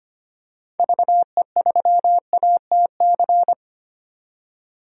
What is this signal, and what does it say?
Morse code, heard as a keyed tone, about 25 wpm, 690 Hz. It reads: VE3ATC
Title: DXCC-100 Call Signs - 6 Characters - 25wpm